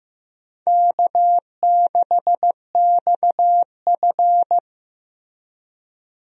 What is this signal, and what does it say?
Morse code, heard as a keyed tone, about 15 wpm, 700 Hz. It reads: K6XF
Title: USA Call Signs - 4 Characters - Repeated 2x - 15wpm